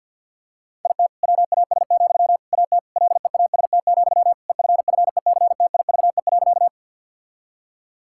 Morse code, heard as a keyed tone, about 50 wpm, 700 Hz. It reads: IT WAS, AT LEAST, EFFECTIVE.
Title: Sentences from Top 1000 Words - Review Part 6 - 50wpm